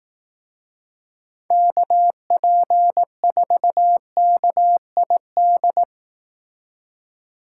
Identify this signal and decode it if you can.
Morse code, heard as a keyed tone, about 18 wpm, 700 Hz. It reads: KP4KID
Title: Speed Racing - USA 2x3 Call Signs 15wpm